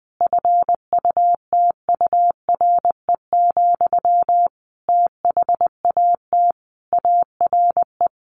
Morse code, heard as a keyed tone, about 20 wpm, 700 Hz. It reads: FUTURE, THAT ARE
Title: A Sound Of Thunder 20wpm